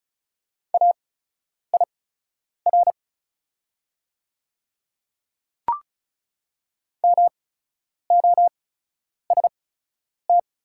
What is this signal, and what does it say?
Morse code, heard as a keyed tone, about 34 wpm, spaced as for 10 wpm, 690 Hz. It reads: AIR MOST
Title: 8x Character Spacing - Top 300 Words - 35wpm